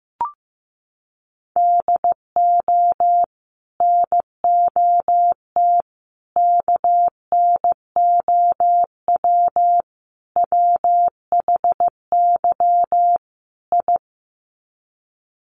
Morse code, heard as a keyed tone, about 15 wpm, 700 Hz. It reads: DO NOT KNOW WHY I
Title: Sets of 5 Words Encore 15wpm